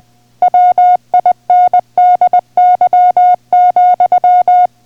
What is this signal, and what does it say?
Morse code, heard as a keyed tone, about 20 wpm, 720 Hz. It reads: WINDY,